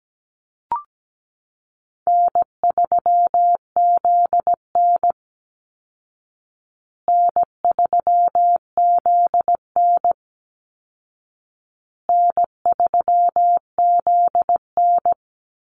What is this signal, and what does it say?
Morse code, heard as a keyed tone, about 17 wpm, 700 Hz. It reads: N3ZN N3ZN N3ZN
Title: Call Signs Repeated 2x - 17wpm